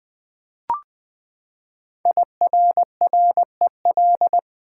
Morse code, heard as a keyed tone, about 20 wpm, 700 Hz. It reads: IRREL
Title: Build A Word - 5 Letters At A Time - 10-Letter Words - 20wpm